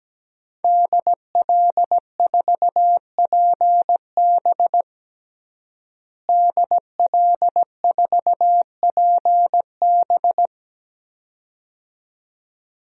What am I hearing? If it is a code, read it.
Morse code, heard as a keyed tone, about 17 wpm, 700 Hz. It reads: DL4PB DL4PB
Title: DXCC-100 Call Signs - 5 Characters - Repeated 2x - 17wpm